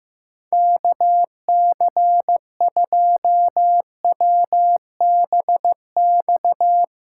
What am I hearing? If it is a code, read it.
Morse code, heard as a keyed tone, about 15 wpm, 700 Hz. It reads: KC2WBX
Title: USA Callsign DE Callsign - 6 Characters - 15wpm